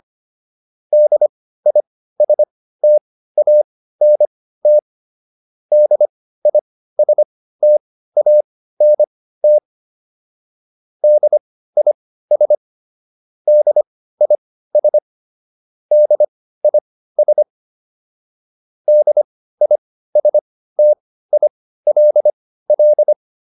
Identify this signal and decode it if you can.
Morse code, heard as a keyed tone, about 25 wpm, spaced as for 15 wpm, 600 Hz. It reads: DISTANT DISTANT DIS DIS DIS DISTILL